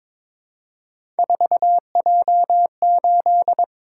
Morse code, heard as a keyed tone, about 22 wpm, 700 Hz. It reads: 4J8
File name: practice / 22wpm / Three Letter-Number Combinations - 22wpm